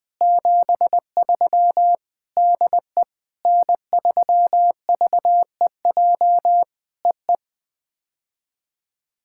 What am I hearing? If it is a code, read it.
Morse code, heard as a keyed tone, about 20 wpm, 700 Hz. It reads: 73 DE N3VEJ EE
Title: POTA QSO Practice - 20wpm